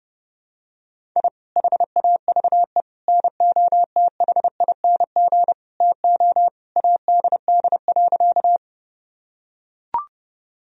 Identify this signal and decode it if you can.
Morse code, heard as a keyed tone, about 30 wpm, 700 Hz. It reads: I HAVE NOTHING TO ADD.
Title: Sentences from Top 1000 Words - Review Part 1 - 30wpm